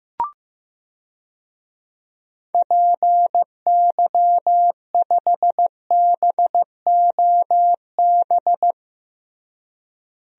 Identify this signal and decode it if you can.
Morse code, heard as a keyed tone, about 16 wpm, 700 Hz. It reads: PY5BOB
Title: DXCC-100 Call Signs - 6 Characters - Repeated 2x - 15wpm